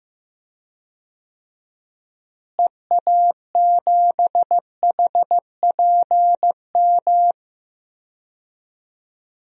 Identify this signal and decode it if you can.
Morse code, heard as a keyed tone, about 15 wpm, 700 Hz. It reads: EA7HPM
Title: DXCC-100 Call Signs - 6 Characters - Repeated 3x - 15wpm